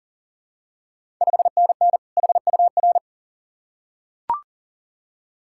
Morse code, heard as a keyed tone, about 40 wpm, 690 Hz. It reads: 5NN HUR
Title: Ontario QSO Party Exchange 40wpm